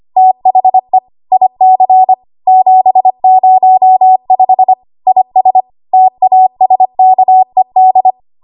Morse code, heard as a keyed tone, about 25 wpm, 750 Hz. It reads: THE IC 705 IS TASKED
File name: Archive / 25 WPM / 25WPM